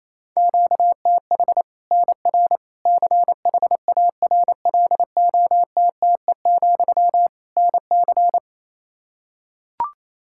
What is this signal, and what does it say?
Morse code, heard as a keyed tone, about 28 wpm, 700 Hz. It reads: QTH NR CHARLOTTE, NC
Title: Practice QSO - Exchange 1 - Beginning - 28wpm